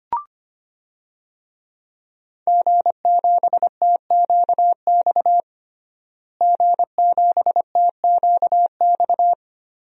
Morse code, heard as a keyed tone, about 25 wpm, 700 Hz. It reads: G7TQX G7TQX
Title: DXCC-100 Call Signs - 5 Characters - Repeated 3x - 25wpm